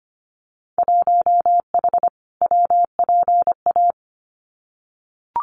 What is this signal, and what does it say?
Morse code, heard as a keyed tone, about 25 wpm, 700 Hz. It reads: 1H WPA
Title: Winter Field Day Exchanges 25wpm